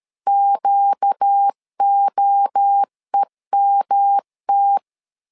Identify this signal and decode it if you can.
Morse code, heard as a keyed tone, about 13 wpm, 800 Hz. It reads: QOEMT
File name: LEZIONI / 4 Q 12wpm 1833